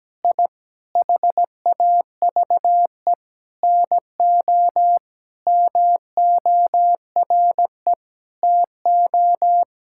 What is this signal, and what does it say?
Morse code, heard as a keyed tone, about 17 wpm, 700 Hz. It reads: I HAVE NO MORE TO
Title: Sentences from Top 1000 Words - Review Part 5 - 17wpm